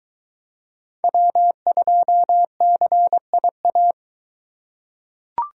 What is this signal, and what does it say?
Morse code, heard as a keyed tone, about 20 wpm, 700 Hz. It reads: W2CIA